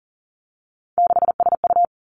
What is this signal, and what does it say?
Morse code, heard as a keyed tone, about 42 wpm, 700 Hz. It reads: 6SU
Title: Three Challenging Letter-Number Combinations 40wpm